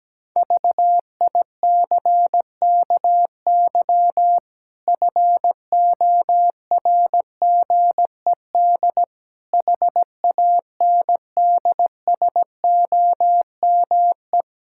Morse code, heard as a keyed tone, about 17 wpm, 700 Hz. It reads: VICKY FORGED HANDSOME
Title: Pangram Challenge v2 - 17wpm